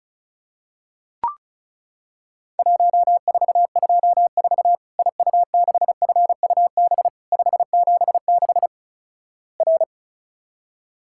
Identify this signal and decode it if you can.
Morse code, heard as a keyed tone, about 36 wpm, 700 Hz. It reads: 1424 IU6FUB 576 R
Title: Worked All Europe DX Contest - QTC Exchange - 35wpm